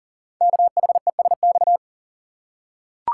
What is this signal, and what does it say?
Morse code, heard as a keyed tone, about 40 wpm, 700 Hz. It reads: KHESX
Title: Code Groups - 5 Letters - 40wpm